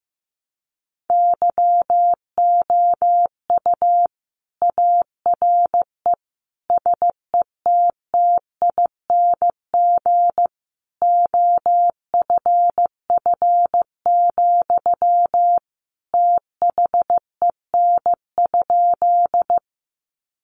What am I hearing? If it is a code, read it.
Morse code, heard as a keyed tone, about 15 wpm, 700 Hz. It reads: YOU ARE SETTING OFF, THEN?
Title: Sentences from Top 1300 Words 15wpm